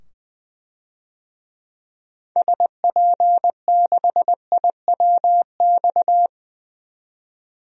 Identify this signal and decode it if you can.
Morse code, heard as a keyed tone, about 21 wpm, 700 Hz. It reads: SP6IWX
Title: DXCC-100 Call Signs - 6 Characters - 20wpm